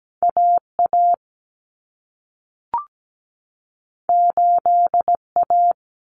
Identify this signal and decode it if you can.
Morse code, heard as a keyed tone, about 17 wpm, 700 Hz. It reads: AA 8A
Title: Common Cut Numbers - 2-Digit 17wpm